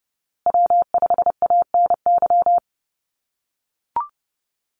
Morse code, heard as a keyed tone, about 30 wpm, 700 Hz. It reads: W5ANY
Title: USA Call Signs Repeated 3x 30wpm